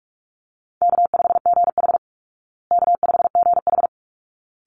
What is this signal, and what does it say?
Morse code, heard as a keyed tone, about 44 wpm, 700 Hz. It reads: K5GH K5GH
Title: Easy Call Signs Repeated 2x - 45wpm